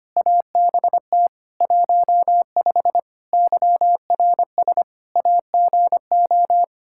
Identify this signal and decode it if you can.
Morse code, heard as a keyed tone, about 25 wpm, 700 Hz. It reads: ABT 15 YRS AGO
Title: Practice QSO - Exchange 3 - More Info - 25wpm